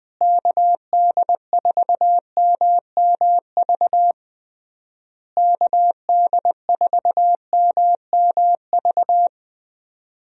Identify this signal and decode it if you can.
Morse code, heard as a keyed tone, about 20 wpm, 700 Hz. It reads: KD4MMV KD4MMV
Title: USA Call Signs - 6 Characters - Repeated 3x - 20wpm